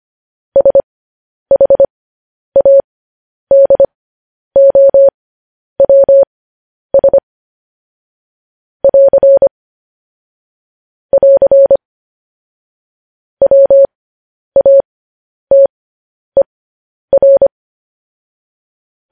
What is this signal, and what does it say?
Morse code, heard as a keyed tone, about 25 wpm, spaced as for 10 wpm, 550 Hz. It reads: SHADOWS <AR> <AR> WATER